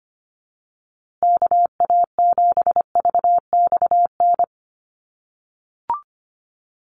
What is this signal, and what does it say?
Morse code, heard as a keyed tone, about 25 wpm, 700 Hz. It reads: KA7VXN
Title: USA Call Signs Repeated 3x 25wpm